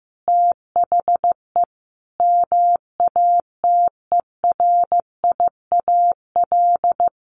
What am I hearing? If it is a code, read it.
Morse code, heard as a keyed tone, about 15 wpm, 700 Hz. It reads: THE MATERIAL
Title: Sentences from Top 1400 Words 15wpm